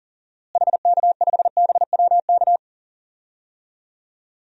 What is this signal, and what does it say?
Morse code, heard as a keyed tone, about 40 wpm, 700 Hz. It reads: HK5BWK
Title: DXCC-100 Call Signs - 6 Characters - Repeated 3x - 40wpm